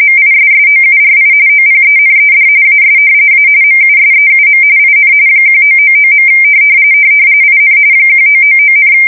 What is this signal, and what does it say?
RTTY, about 45.45 baud. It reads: MISSOURI OPENED AS A MUSEUM IN 1999. IN 1999,
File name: fsk170 / battleshipnj